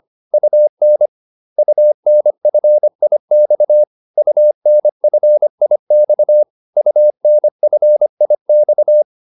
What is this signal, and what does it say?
Morse code, heard as a keyed tone, about 25 wpm, 600 Hz. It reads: UN UNFIX UNFIX UNFIX